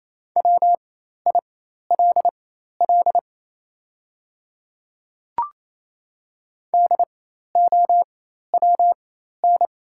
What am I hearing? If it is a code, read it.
Morse code, heard as a keyed tone, about 28 wpm, spaced as for 13 wpm, 700 Hz. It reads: WILL DOWN
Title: 4x Character Spacing - Top 300 Words - 28wpm